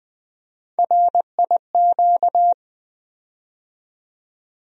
Morse code, heard as a keyed tone, about 20 wpm, 700 Hz. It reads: RIQ
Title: USA Call Sign Postfixes - 3 Letters 20wpm